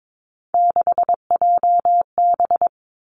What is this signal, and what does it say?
Morse code, heard as a keyed tone, about 22 wpm, 700 Hz. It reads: 6JB
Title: Three Challenging Letter-Number Combinations 22wpm